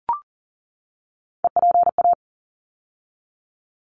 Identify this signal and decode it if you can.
Morse code, heard as a keyed tone, about 38 wpm, 700 Hz. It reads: EPA
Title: ARRL Sections 40wpm